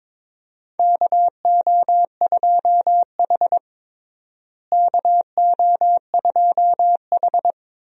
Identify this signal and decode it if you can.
Morse code, heard as a keyed tone, about 22 wpm, 700 Hz. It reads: KO2H KO2H
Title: USA Call Signs - 4 Characters - Repeated 3x - 22wpm